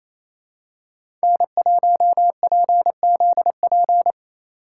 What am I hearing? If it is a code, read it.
Morse code, heard as a keyed tone, about 28 wpm, 700 Hz. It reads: N1PZP
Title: USA Call Signs - 5 Characters - 2x Character Spacing - 28wpm